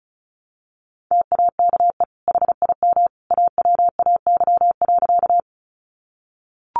Morse code, heard as a keyed tone, about 34 wpm, 700 Hz. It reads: TAKE HIM AWAY.
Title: Sentences from Top 600 Words 35wpm